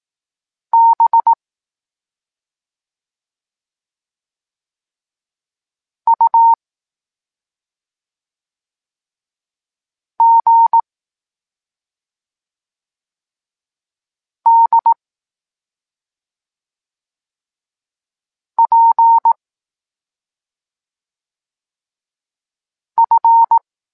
Morse code, heard as a keyed tone, about 18 wpm, 920 Hz. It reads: B U G D P F
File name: Tekst-26 1